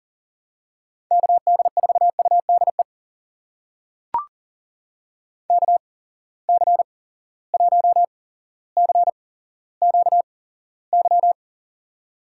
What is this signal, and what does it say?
Morse code, heard as a keyed tone, about 40 wpm, spaced as for 17 wpm, 700 Hz. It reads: KD4UDE KC1CQY
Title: USA Call Signs - 6 Characters - 8x Character Spacing - 40wpm